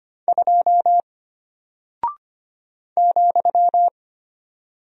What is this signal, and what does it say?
Morse code, heard as a keyed tone, about 25 wpm, 700 Hz. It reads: 2 ,
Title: Single Character (Letter:Number:Punctuation:Prosign) Alt 25wpm